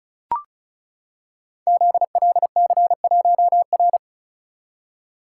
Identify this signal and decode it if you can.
Morse code, heard as a keyed tone, about 36 wpm, 690 Hz. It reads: ZLC1R
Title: Code Groups - 5 Letter:Numbers - 35wpm